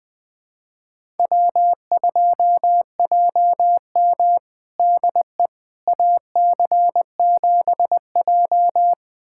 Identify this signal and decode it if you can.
Morse code, heard as a keyed tone, about 21 wpm, 700 Hz. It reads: W2JM DE AC7J
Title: USA Callsign DE Callsign 20wpm